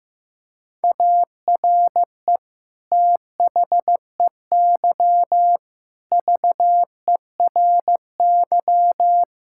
Morse code, heard as a keyed tone, about 16 wpm, 700 Hz. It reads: ARE THEY VERY
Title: Sentences from Top 1000 Words - Review Part 5 - 15wpm